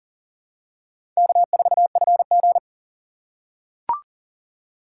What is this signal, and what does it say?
Morse code, heard as a keyed tone, about 40 wpm, 700 Hz. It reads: K4FG
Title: USA Call Signs - 4x Character Spacing - 40wpm